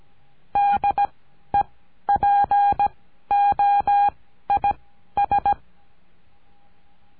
Morse code, heard as a keyed tone, about 17 wpm, spaced as for 12 wpm, 810 Hz. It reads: DEPOIS
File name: CURSO DE TELEGRAFIA